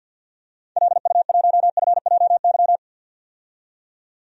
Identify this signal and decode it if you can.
Morse code, heard as a keyed tone, about 47 wpm, 700 Hz. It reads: LU1FJY
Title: DXCC-100 Call Signs - 6 Characters - 50wpm